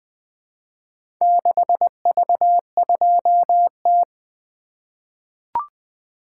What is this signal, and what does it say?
Morse code, heard as a keyed tone, about 20 wpm, 700 Hz. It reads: 6V2T
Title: Four Letter-Number Combinations - 20wpm